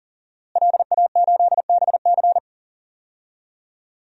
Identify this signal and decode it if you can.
Morse code, heard as a keyed tone, about 37 wpm, 700 Hz. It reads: LA8BC
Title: DXCC-100 Call Signs - 5 Characters - Repeated 2x - 40wpm